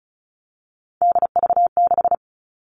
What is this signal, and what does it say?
Morse code, heard as a keyed tone, about 34 wpm, 700 Hz. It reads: DV6
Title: All Cut Numbers - 3-Digit 35wpm